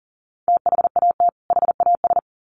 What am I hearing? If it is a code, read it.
Morse code, heard as a keyed tone, about 40 wpm, 700 Hz. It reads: THAT HAS
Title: Sets of 2 Words 40wpm